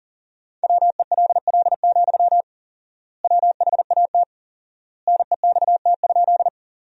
Morse code, heard as a keyed tone, about 40 wpm, 700 Hz. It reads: WELL, WHAT NEXT?
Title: 4x Word-Spacing - Sentences from Top 300 Words - 40wpm